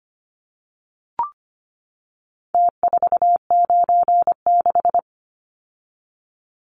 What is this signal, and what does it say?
Morse code, heard as a keyed tone, about 25 wpm, 700 Hz. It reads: T496
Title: Numbers - Contest Style - 4-Digit 25wpm